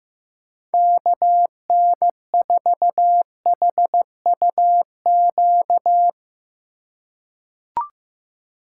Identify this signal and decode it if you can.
Morse code, heard as a keyed tone, about 15 wpm, 700 Hz. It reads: KN4HUQ
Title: USA Call Signs - 6 Characters - 2x Character Spacing - 15wpm